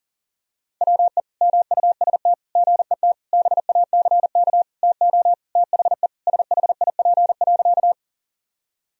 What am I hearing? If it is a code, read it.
Morse code, heard as a keyed tone, about 37 wpm, 700 Hz. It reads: WE MUST GET BACK TO THE SHIP.
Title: Sentences from Top 1000 Words - Review Part 2 - 40wpm